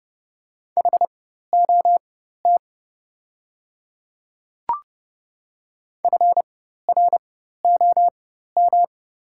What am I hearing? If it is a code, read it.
Morse code, heard as a keyed tone, about 30 wpm, spaced as for 14 wpm, 700 Hz. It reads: HOT FROM
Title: 4x Character Spacing - Top 100 Words - 30wpm